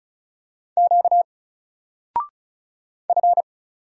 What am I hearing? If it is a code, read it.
Morse code, heard as a keyed tone, about 35 wpm, 700 Hz. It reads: Q F
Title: Single Letter-Number Alt 35wpm